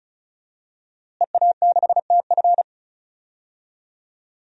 Morse code, heard as a keyed tone, about 36 wpm, 700 Hz. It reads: EA6TF
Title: DXCC-100 Call Signs - 5 Characters - Repeated 3x - 35wpm